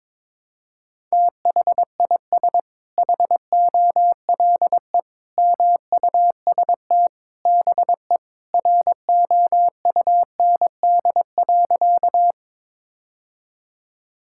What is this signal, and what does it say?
Morse code, heard as a keyed tone, about 22 wpm, 700 Hz. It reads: THIS HOLE MUST BE ROUND.
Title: Sentences from Top 1600 Words 22wpm